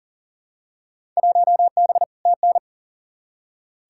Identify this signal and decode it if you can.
Morse code, heard as a keyed tone, about 40 wpm, 700 Hz. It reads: 1B TN
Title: Field Day Exchanges - 40wpm